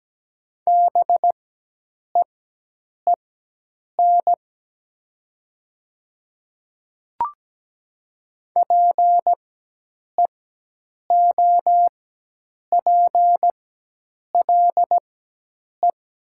Morse code, heard as a keyed tone, about 17 wpm, spaced as for 8 wpm, 700 Hz. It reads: BEEN PEOPLE
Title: 4x Character Spacing - Top 100 Words - 17wpm